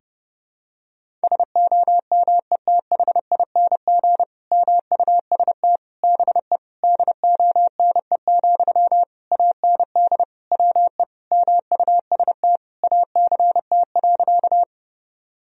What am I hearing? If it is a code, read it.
Morse code, heard as a keyed tone, about 30 wpm, 700 Hz. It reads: SOMETHING MUST BE DONE, AND WE MUST ACT.
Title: Sentences from Top 1000 Words - Review Part 1 - 30wpm